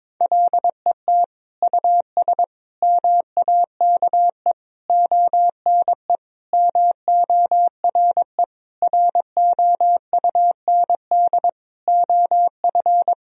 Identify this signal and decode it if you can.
Morse code, heard as a keyed tone, about 22 wpm, 700 Hz. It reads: LET US MAKE ONE MORE ROUND OF
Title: Sentences from Top 1000 Words - Review Part 4 - 22wpm